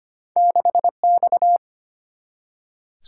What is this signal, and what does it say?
Morse code, heard as a keyed tone, about 33 wpm, 700 Hz. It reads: 6X